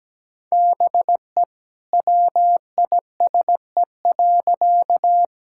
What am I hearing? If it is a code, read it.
Morse code, heard as a keyed tone, about 17 wpm, 700 Hz. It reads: BE WISE.
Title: Sentences from Top 1600 Words 17wpm